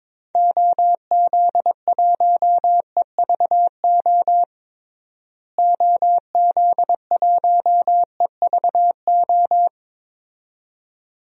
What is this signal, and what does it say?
Morse code, heard as a keyed tone, about 22 wpm, 700 Hz. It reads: OZ1EVO OZ1EVO
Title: DXCC-100 Call Signs - 6 Characters - Repeated 2x - 22wpm